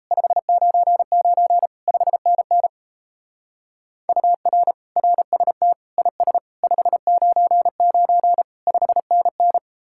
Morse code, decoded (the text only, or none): 599 5NN UR RST IS 599 5NN